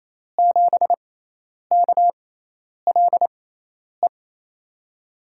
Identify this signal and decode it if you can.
Morse code, heard as a keyed tone, about 28 wpm, spaced as for 10 wpm, 700 Hz. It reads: 7KLE
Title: USA Call Signs - 6 Characters - 6x Character Spacing - 28wpm